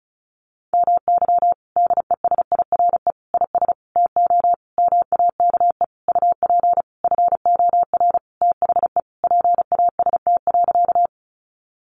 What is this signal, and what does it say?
Morse code, heard as a keyed tone, about 34 wpm, 700 Hz. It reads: MY DESIRE IS TO MAKE UP FOR THE PAST.
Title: Sentences from Top 1100 Words 35wpm